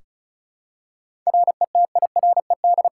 Morse code, heard as a keyed tone, about 36 wpm, 700 Hz. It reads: RETIRED
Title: Practice QSO - Exchange 3 - More Info - 35wpm